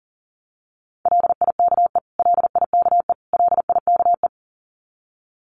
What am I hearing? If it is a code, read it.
Morse code, heard as a keyed tone, about 40 wpm, 700 Hz. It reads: LIKE LIKE LIKE
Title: Top 300 Words Repeated 40wpm